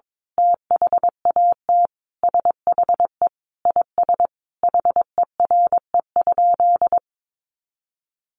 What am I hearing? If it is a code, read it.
Morse code, heard as a keyed tone, about 22 wpm, 700 Hz. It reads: THAT SHE IS HERE?
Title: Sentences from Top 1300 Words 22wpm